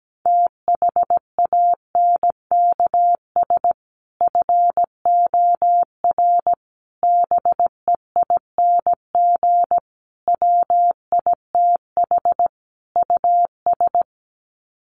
Morse code, heard as a keyed tone, about 17 wpm, 700 Hz. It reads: THANKS FOR BEING WITH US
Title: Sets of 5 Words 17wpm